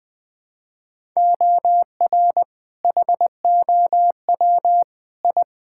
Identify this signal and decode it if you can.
Morse code, heard as a keyed tone, about 20 wpm, 700 Hz. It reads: OR HOW I
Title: Sets of 8 Words - 20wpm